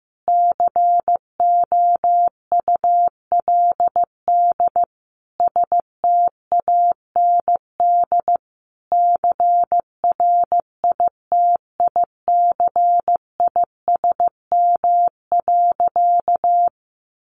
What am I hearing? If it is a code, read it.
Morse code, heard as a keyed tone, about 15 wpm, 700 Hz. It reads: COULD STAND CRITICISM.
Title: Sentences from Top 1300 Words 15wpm